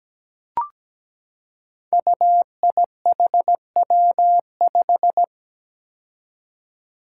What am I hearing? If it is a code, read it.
Morse code, heard as a keyed tone, about 17 wpm, 700 Hz. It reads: UIHW5
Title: Five Challenging Letter-Number Combinations 17wpm